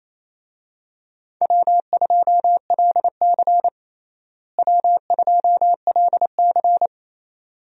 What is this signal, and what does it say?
Morse code, heard as a keyed tone, about 28 wpm, 700 Hz. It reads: W2LC W2LC
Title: USA Call Signs - 4 Characters - Repeated 2x - 28wpm